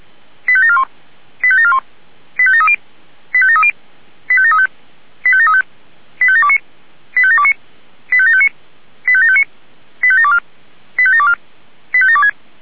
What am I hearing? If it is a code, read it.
Five-tone selective call (ZVEI), heard as quick runs of five tones, tones of about 70 ms. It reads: 85741 85741 85730 85730 85735 85735 85729 85729 85759 85759 85724 85724 85726